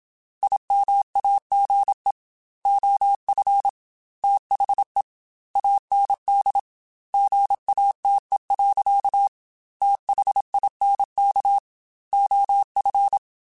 Morse code, heard as a keyed tone, about 27 wpm, 800 Hz. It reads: IMAGE OF THE AND GATE. THINK OF
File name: Technical Subjects / Technical Subjects-mp3s / OR Gate @25